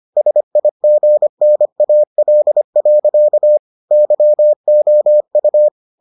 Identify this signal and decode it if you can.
Morse code, heard as a keyed tone, about 25 wpm, 600 Hz. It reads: SIGNAL. YOU